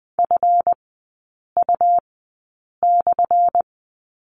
Morse code, heard as a keyed tone, about 20 wpm, spaced as for 8 wpm, 700 Hz. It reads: FU/
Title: 178 Lesson 29e - Slash - Three Characters 20wpm